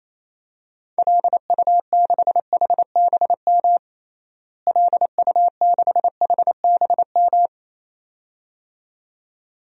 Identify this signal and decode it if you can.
Morse code, heard as a keyed tone, about 28 wpm, 700 Hz. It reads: LU6HBM LU6HBM
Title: DXCC-100 Call Signs - 6 Characters - Repeated 2x - 28wpm